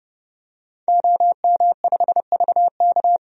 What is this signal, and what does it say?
Morse code, heard as a keyed tone, about 30 wpm, 700 Hz. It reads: OM5VK
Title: DXCC-100 Call Signs - 5 Characters - 30wpm